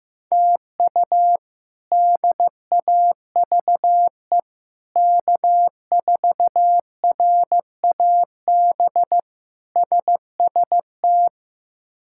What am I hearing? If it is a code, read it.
Morse code, heard as a keyed tone, about 15 wpm, 700 Hz. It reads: TU DAVE K4RAB SST
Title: K1USN SST QSO Practice - 15wpm